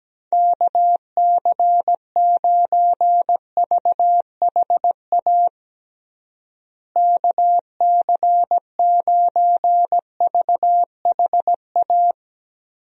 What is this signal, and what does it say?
Morse code, heard as a keyed tone, about 17 wpm, 700 Hz. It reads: KC9VHA KC9VHA
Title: USA Call Signs - 6 Characters - Repeated 3x - 17wpm